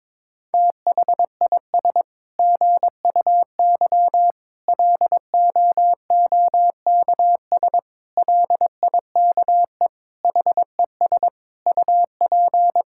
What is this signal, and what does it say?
Morse code, heard as a keyed tone, about 22 wpm, 700 Hz. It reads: THIS GUY LOOKS LIKE HES UP
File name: Sets of 9 Words - 22wpm